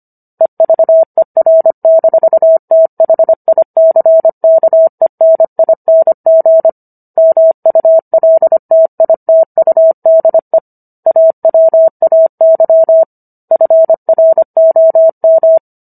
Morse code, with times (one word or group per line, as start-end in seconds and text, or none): 0.4-6.7 EVER-THICKENING
7.2-10.6 MULTITUDE
11.1-13.0 AWAY
13.5-15.6 FROM